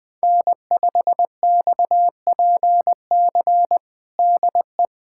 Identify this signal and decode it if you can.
Morse code, heard as a keyed tone, about 20 wpm, 700 Hz. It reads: N5XPC DE
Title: USA Callsign DE Callsign - 5 Characters - 20wpm